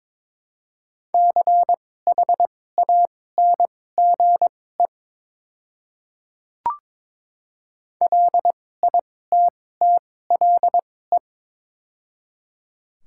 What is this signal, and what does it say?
Morse code, heard as a keyed tone, about 22 wpm, spaced as for 16 wpm, 700 Hz. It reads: CHANGE LITTLE
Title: 2x Character Spacing - Top 200 Words - 22wpm